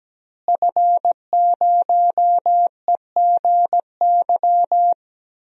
Morse code, heard as a keyed tone, about 17 wpm, 700 Hz. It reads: F0EGY
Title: DXCC-100 Call Signs - 5 Characters - Repeated 2x - 17wpm